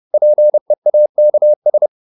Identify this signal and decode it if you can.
Morse code, heard as a keyed tone, about 30 wpm, 600 Hz. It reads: PEAKS